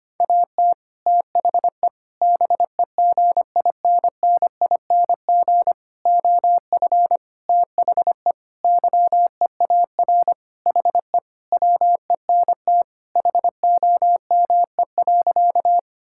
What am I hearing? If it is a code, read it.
Morse code, heard as a keyed tone, about 25 wpm, 700 Hz. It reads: AT THE BEGINNING OF THE YEAR HE WENT HOME.
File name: practice / 25wpm / Sentences with 4-Word Phrases - 25wpm